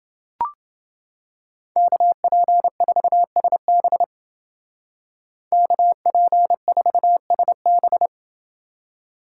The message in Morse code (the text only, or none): KP4SB KP4SB